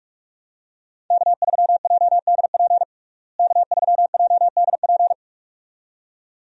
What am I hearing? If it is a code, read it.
Morse code, heard as a keyed tone, about 42 wpm, 700 Hz. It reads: K3JDP K3JDP
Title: USA Call Signs - 5 Characters - Repeated 2x - 45wpm